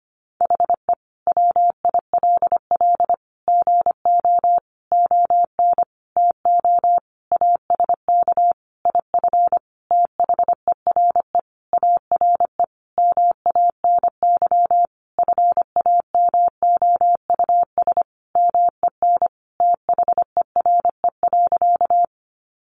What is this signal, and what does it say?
Morse code, heard as a keyed tone, about 25 wpm, 700 Hz. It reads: HE WILL GO ON TO ASK IF THERE ARE MANY FAMOUS MEN THERE.
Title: Sentences from Top 1500 Words 25wpm